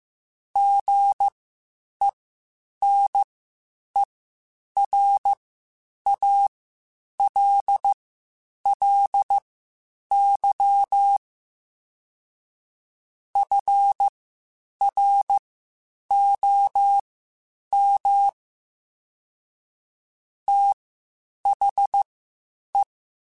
Morse code, heard as a keyed tone, about 15 wpm, spaced as for 8 wpm, 800 Hz. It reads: GENERALLY FROM THE
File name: Spaced Characters / Spaced Characters-mp3s / Weather 5 12 sp @13